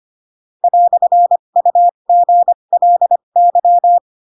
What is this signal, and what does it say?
Morse code, heard as a keyed tone, about 25 wpm, spaced as for 22 wpm, 700 Hz. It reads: "UGLY
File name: Pinocchio / Pinocchio ch2 18 25 ik1hge